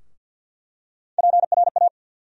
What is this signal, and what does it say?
Morse code, heard as a keyed tone, about 47 wpm, 700 Hz. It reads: PRA
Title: USA Call Sign Postfixes - 3 Letters 50wpm